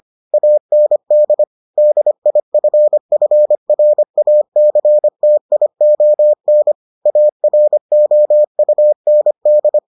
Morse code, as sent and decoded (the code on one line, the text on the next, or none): .- -. -.. / -.. .. ..-. ..-. .-. .- -.-. - .. --- -. / .- .-. --- ..- -. -..
AND DIFFRACTION AROUND